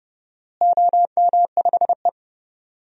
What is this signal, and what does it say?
Morse code, heard as a keyed tone, about 32 wpm, 700 Hz. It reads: OM5E
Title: DXCC-100 Call Signs - 4 Characters - Repeated 3x - 30wpm